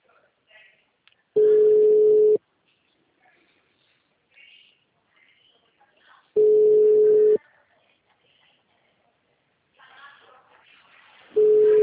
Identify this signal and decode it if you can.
Ringback tone (Europe, 425 Hz).